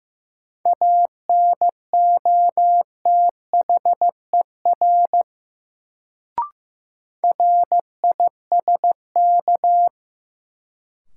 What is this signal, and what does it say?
Morse code, heard as a keyed tone, about 15 wpm, 700 Hz. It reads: ANOTHER RISK
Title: Sentences from Top 600 Words - New Word Drill - 15wpm